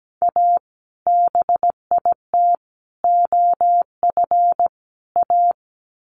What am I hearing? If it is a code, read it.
Morse code, heard as a keyed tone, about 17 wpm, 700 Hz. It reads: A BIT OF A
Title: Sets of 4 Words 17wpm